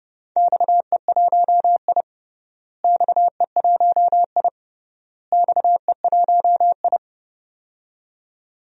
Morse code, heard as a keyed tone, about 32 wpm, 700 Hz. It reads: XE1I XE1I XE1I
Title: DXCC-100 Call Signs - 4 Characters - Repeated 3x - 30wpm